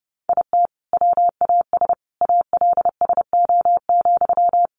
Morse code, heard as a keyed tone, about 30 wpm, 700 Hz. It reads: IT WAS ALSO,
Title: Sentences from Top 1400 Words 30wpm